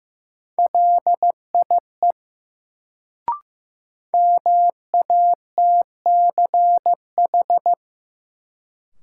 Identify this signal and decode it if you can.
Morse code, heard as a keyed tone, about 15 wpm, 700 Hz. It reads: LIE MATCH
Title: Sentences from Top 1600 Words - New Word Drill Alt - 15wpm